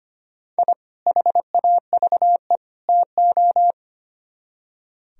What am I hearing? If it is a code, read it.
Morse code, heard as a keyed tone, about 29 wpm, 700 Hz. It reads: I HAVE TO
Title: Speed-Racing - Sets of 3 Words 25wpm